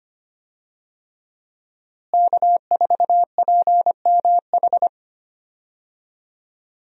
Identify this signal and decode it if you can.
Morse code, heard as a keyed tone, about 25 wpm, 700 Hz. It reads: K4PMH
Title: USA Call Signs - 5 Characters - Repeated 3x - 25wpm